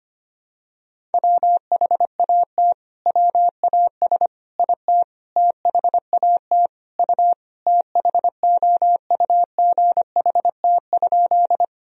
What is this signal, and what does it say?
Morse code, heard as a keyed tone, about 25 wpm, 700 Hz. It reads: WHAT WAS IT THAT U THOUGHT?
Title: Sentences from Top 300 Words and Abbreviations 25wpm